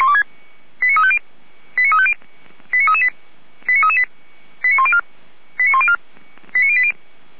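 Five-tone selective call (ZVEI), heard as quick runs of five tones, tones of about 70 ms. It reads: 89350 89350 89338 89338 89104 89104 89080
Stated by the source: ZVEI five-tone call received by SDR, from the Feuerwehr Orlamünde audio archive.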